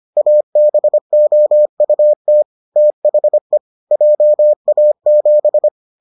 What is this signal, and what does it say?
Morse code, heard as a keyed tone, about 25 wpm, 600 Hz. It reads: ABOUT THE JA7